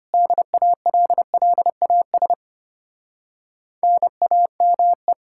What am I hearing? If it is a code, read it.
Morse code, heard as a keyed tone, about 33 wpm, 700 Hz. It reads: DALLAS NAME